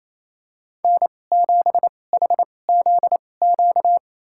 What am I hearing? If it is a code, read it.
Morse code, heard as a keyed tone, about 28 wpm, spaced as for 20 wpm, 700 Hz. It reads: N7HZQ
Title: USA Call Signs - 5 Characters - 2x Character Spacing - 28wpm